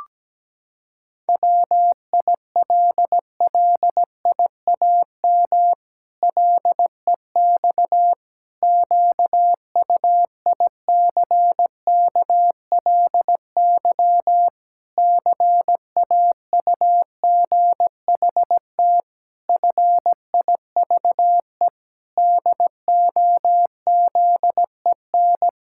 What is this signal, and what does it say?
Morse code, heard as a keyed tone, about 17 wpm, 700 Hz. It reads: WILLIAM LEX QUICKLY CAUGHT FIVE DOZEN
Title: Pangram Challenge v2 - 17wpm